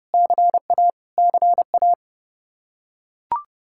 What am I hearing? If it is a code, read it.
Morse code, heard as a keyed tone, about 30 wpm, 700 Hz. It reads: CA CA
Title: Practice QSO - Exchange 1 - Beginning - 30wpm